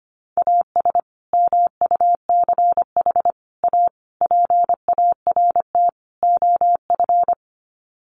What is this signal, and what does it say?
Morse code, heard as a keyed tone, about 25 wpm, 700 Hz. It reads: AS MUCH A PART OF
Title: Sets of 5 Words Encore2 25wpm